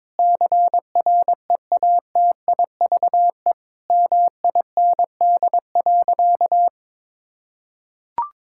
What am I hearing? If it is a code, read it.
Morse code, heard as a keyed tone, about 22 wpm, 700 Hz. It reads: CREATIVE MIND.
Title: Sentences from Top 1000 Words - Review Part 5 - 22wpm